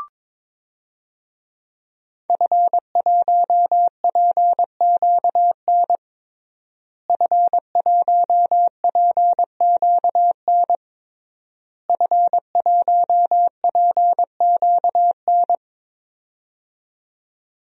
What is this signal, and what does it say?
Morse code, heard as a keyed tone, about 22 wpm, 700 Hz. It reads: F1PQN F1PQN F1PQN
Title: DXCC-100 Call Signs - 5 Characters - Repeated 3x - 22wpm